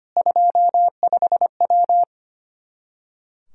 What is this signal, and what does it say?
Morse code, heard as a keyed tone, about 25 wpm, 700 Hz. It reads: 25W
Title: K1USN SST QSO Practice 25wpm